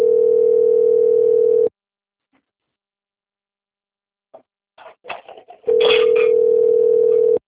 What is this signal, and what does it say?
Ringback tone (North America), 440 + 480 Hz.
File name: out- - - - - .6188